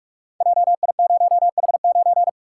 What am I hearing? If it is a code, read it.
Morse code, heard as a keyed tone, about 42 wpm, 700 Hz. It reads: JI0H9
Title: Code Groups - 5 Letter:Numbers - 45wpm